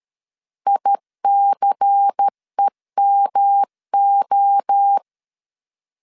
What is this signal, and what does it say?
Morse code, heard as a keyed tone, about 13 wpm, 800 Hz. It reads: ICEMO
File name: LEZIONI / 6 PKX0 / 6 PKX0 12wpm 2231